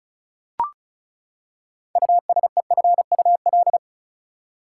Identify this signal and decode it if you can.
Morse code, heard as a keyed tone, about 34 wpm, 700 Hz. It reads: USEFUL
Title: Six Letter Words - 35wpm